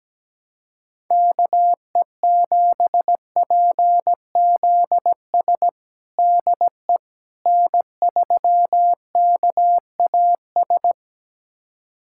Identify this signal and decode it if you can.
Morse code, heard as a keyed tone, about 17 wpm, 700 Hz. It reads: KE7PZS DE N3KAS
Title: USA Callsign DE Callsign 17wpm